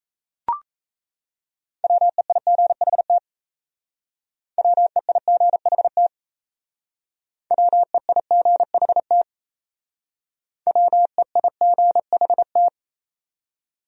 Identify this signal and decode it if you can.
Morse code, heard as a keyed tone, about 36 wpm, 700 Hz. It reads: WEIGHT WEIGHT WEIGHT WEIGHT